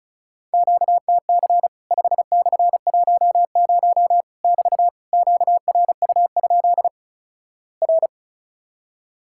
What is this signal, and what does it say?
Morse code, heard as a keyed tone, about 36 wpm, 700 Hz. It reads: QTC 5/10 <BT> QRU? R
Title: Worked All Europe DX Contest - QTC Exchange - 35wpm